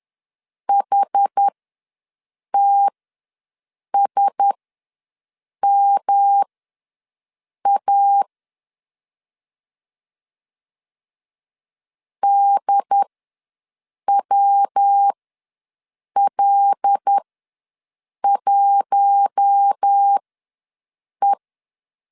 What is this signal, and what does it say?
Morse code, heard as a keyed tone, about 10 wpm, spaced as for 6 wpm, 800 Hz. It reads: HTSMA DWL1E
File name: LEZIONI / 5 LBVC / 5 L ws 10wpm 3636